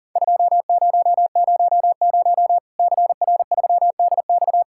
Morse code, heard as a keyed tone, about 40 wpm, 700 Hz. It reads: 2000 CR3DX
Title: Worked All Europe DX Contest - QTC Exchange - 40wpm